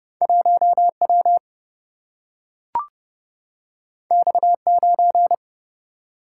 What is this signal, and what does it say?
Morse code, heard as a keyed tone, about 30 wpm, 700 Hz. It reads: 1W X9
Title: Single-Letter Single-Number 30wpm